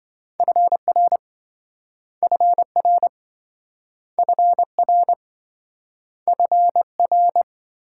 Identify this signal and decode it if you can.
Morse code, heard as a keyed tone, about 28 wpm, 700 Hz. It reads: FR FR FR FR